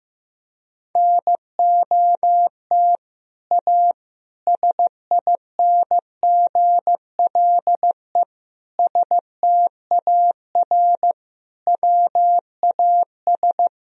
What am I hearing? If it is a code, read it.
Morse code, heard as a keyed tone, about 15 wpm, 700 Hz. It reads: NOT A SINGLE STAR WAS
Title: Sentences from Top 1000 Words - Review Part 4 - 15wpm